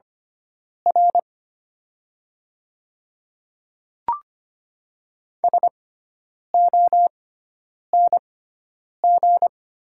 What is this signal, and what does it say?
Morse code, heard as a keyed tone, about 25 wpm, spaced as for 9 wpm, 700 Hz. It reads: R SONG
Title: 6x Character Spacing - Top 500 Words - 25wpm